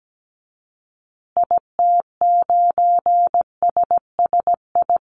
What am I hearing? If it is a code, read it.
Morse code, heard as a keyed tone, about 17 wpm, 700 Hz. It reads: IT9SSI
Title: Call Signs Repeated 2x - 17wpm